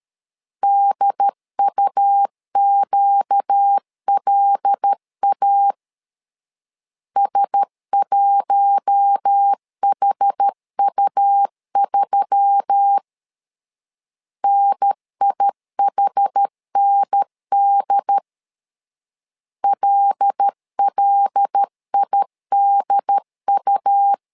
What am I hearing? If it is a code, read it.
Morse code, heard as a keyed tone, about 13 wpm, 800 Hz. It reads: DUQLA S1HU3 NIHND LLIDU
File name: LEZIONI / 5 LBVC / 5 L 12wpm 1913